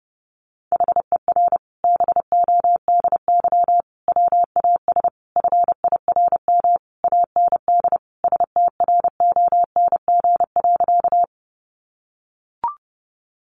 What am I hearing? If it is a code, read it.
Morse code, heard as a keyed tone, about 30 wpm, 700 Hz. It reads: HER BODY WAS FIRM AND STRONG.
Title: Sentences from Top 1400 Words 30wpm